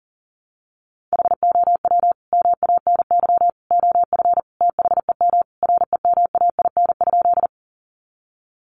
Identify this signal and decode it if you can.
Morse code, heard as a keyed tone, about 40 wpm, 700 Hz. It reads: HOW MANY OF THEM REMAIN?
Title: Sentences from Top 1200 Words 40wpm